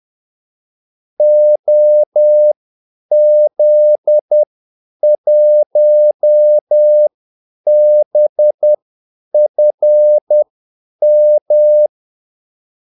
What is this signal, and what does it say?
Morse code, heard as a keyed tone, about 10 wpm, spaced as for 8 wpm, 600 Hz. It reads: OZ1BFM
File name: Callsigns-10-8wpm